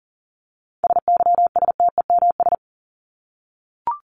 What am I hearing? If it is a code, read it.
Morse code, heard as a keyed tone, about 40 wpm, 700 Hz. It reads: SYSTEMS
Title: ES:S Suffix - Top 100 Words with Base Words 40wpm